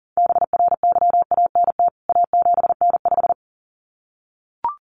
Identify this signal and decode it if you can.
Morse code, heard as a keyed tone, about 40 wpm, 700 Hz. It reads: BRYANT A7N5
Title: CWT Exchange 40wpm